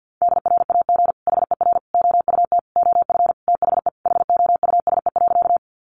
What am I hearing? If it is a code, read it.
Morse code, heard as a keyed tone, about 50 wpm, 700 Hz. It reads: DRAG HER OUT OF THE HOUSE.
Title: Sentences from Top 1500 Words 50wpm